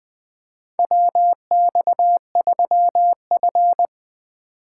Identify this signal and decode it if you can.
Morse code, heard as a keyed tone, about 20 wpm, 700 Hz. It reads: WX3F
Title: USA Call Signs - 4 Characters - 3x Character Spacing - 20wpm